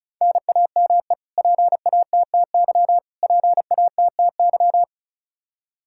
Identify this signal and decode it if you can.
Morse code, heard as a keyed tone, about 36 wpm, 700 Hz. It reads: NAME PATTY PATTY
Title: Practice QSO - Exchange 1 - Beginning - 35wpm